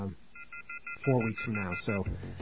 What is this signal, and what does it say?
Off-hook warning tone, 1400 + 2060 + 2450 + 2600 Hz, with speech.